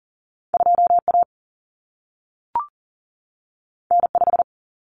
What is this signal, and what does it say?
Morse code, heard as a keyed tone, about 40 wpm, 700 Hz. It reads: 2A N5
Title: Common Cut Numbers - 2-Digit 40wpm